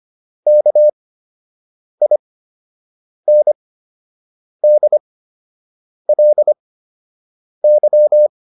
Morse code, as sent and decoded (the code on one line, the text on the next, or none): -.- .. -. -.. .-.. -.--
KINDLY